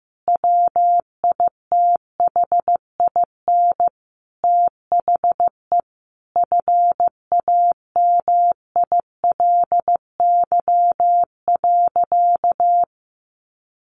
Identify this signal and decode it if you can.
Morse code, heard as a keyed tone, about 14 wpm, 700 Hz. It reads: WITHIN THE FAMILY.
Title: Sentences from Top 1500 Words 15wpm